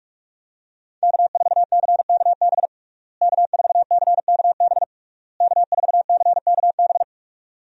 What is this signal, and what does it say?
Morse code, heard as a keyed tone, about 46 wpm, 700 Hz. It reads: K4CKB K4CKB K4CKB
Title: USA Call Signs - 5 Characters - Repeated 3x - 45wpm